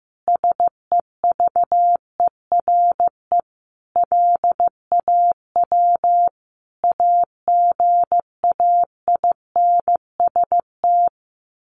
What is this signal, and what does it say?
Morse code, heard as a keyed tone, about 15 wpm, 700 Hz. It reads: SEVERE LAW AGAINST
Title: Sentences from Top 1400 Words 15wpm